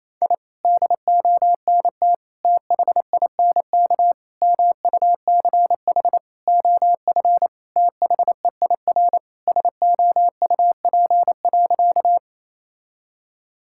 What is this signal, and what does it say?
Morse code, heard as a keyed tone, about 27 wpm, 690 Hz. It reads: I DONT THINK MUCH OF THEIR SOUP.
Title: Sentences from Top 1000 Words - Review Part 6 - 28wpm